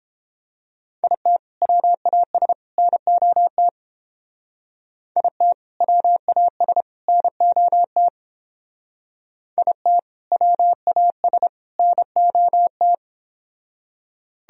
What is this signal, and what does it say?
Morse code, heard as a keyed tone, about 28 wpm, 700 Hz. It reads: IT WAS NOT IT WAS NOT IT WAS NOT